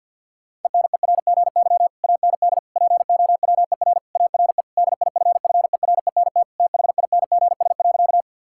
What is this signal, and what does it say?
Morse code, heard as a keyed tone, about 50 wpm, 700 Hz. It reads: ENERGY AND POWER ARE DIFFERENT THINGS.
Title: Sentences from Top 1000 Words - Review Part 6 - 50wpm